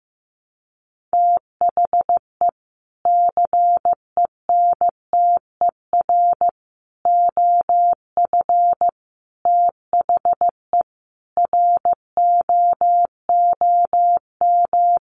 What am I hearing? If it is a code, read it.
Morse code, heard as a keyed tone, about 15 wpm, 700 Hz. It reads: THE CENTER OF THE ROOM
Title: Sets of 5 Words Encore 15wpm